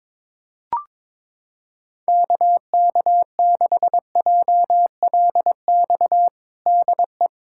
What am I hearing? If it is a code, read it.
Morse code, heard as a keyed tone, about 22 wpm, 700 Hz. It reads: KK6JLX DE
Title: USA Callsign DE Callsign 22wpm